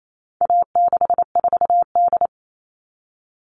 Morse code, heard as a keyed tone, about 28 wpm, 700 Hz. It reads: A64D
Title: All Cut Numbers - 4-Digit 28wpm